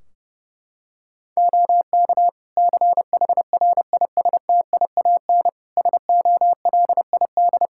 Morse code, heard as a keyed tone, about 30 wpm, 700 Hz. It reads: OK CHRISTIAN SOLID
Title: Practice QSO - Exchange 3 - More Info - 30wpm